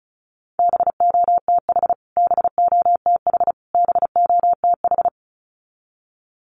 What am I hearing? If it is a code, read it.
Morse code, heard as a keyed tone, about 34 wpm, 700 Hz. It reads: BOTH BOTH BOTH
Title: Top 500 Words Repeated 35wpm